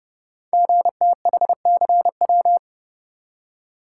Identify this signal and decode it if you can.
Morse code, heard as a keyed tone, about 29 wpm, 700 Hz. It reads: GTHCW
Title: Code Groups - 5 Letters - 30wpm